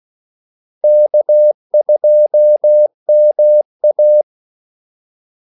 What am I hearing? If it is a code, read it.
Morse code, heard as a keyed tone, about 16 wpm, 600 Hz. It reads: K2MA
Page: Kaldesignaler 16 wpm